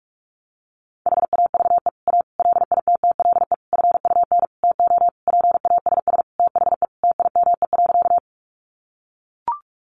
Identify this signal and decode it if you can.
Morse code, heard as a keyed tone, about 44 wpm, 700 Hz. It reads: HAVE A LITTLE FUN TO PASS THE TIME.
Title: Sentences from Top 700 Words 45wpm